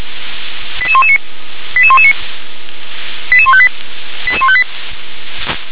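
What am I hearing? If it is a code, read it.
Five-tone selective call (ZVEI), heard as quick runs of five tones, tones of about 70 ms. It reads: 88119 88119 88158 88158
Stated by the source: Five-tone ZVEI selective call received by SDR.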